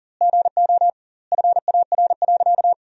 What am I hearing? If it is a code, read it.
Morse code, heard as a keyed tone, about 40 wpm, 700 Hz. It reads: GO FAR.
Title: 2x Word-Spacing - Sentences from Top 400 Words - 40wpm